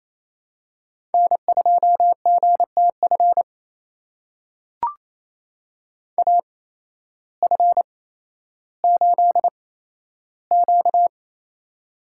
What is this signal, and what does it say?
Morse code, heard as a keyed tone, about 28 wpm, spaced as for 13 wpm, 700 Hz. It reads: N2GTF AF8Q
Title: USA Call Signs - 8x Character Spacing - 28wpm